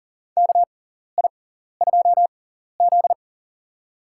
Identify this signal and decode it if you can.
Morse code, heard as a keyed tone, about 40 wpm, spaced as for 14 wpm, 700 Hz. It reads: KI2Z
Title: USA Call Signs - 4 Characters - 6x Character Spacing - 40wpm